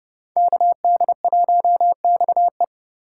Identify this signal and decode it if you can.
Morse code, heard as a keyed tone, about 30 wpm, 700 Hz. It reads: KD1XE